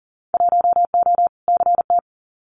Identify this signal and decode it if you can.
Morse code, heard as a keyed tone, about 40 wpm, 700 Hz. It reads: 1O CT
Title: Winter Field Day Exchanges 40wpm